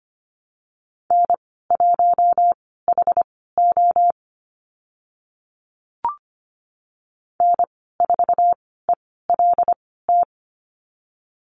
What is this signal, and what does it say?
Morse code, heard as a keyed tone, about 25 wpm, spaced as for 16 wpm, 700 Hz. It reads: N1HO N4ELT
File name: Lesson 15g - Number 5 - Call Signs 25wpm